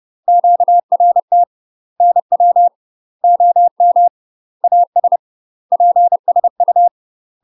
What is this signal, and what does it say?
Morse code, heard as a keyed tone, about 30 wpm, 700 Hz. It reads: QRT NW OM AS PSU
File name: UK MASTER QSOs TRAINING / 30WPM / UK MASTER QSOs 0042